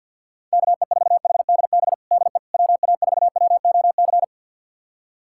Morse code, heard as a keyed tone, about 47 wpm, 700 Hz. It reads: KE4HDB DE WA4WOC
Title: USA Callsign DE Callsign - 6 Characters - 50wpm